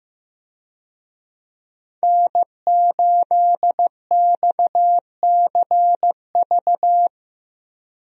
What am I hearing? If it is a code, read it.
Morse code, heard as a keyed tone, about 15 wpm, 700 Hz. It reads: N8XCV
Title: USA Call Signs - 5 Characters - Repeated 3x - 15wpm